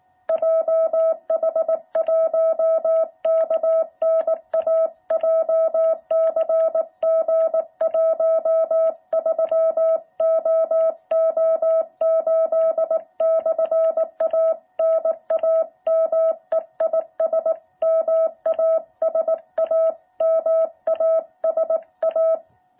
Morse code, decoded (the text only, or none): JH1KNA JCG 13OO8/A NAME IS MASA MASA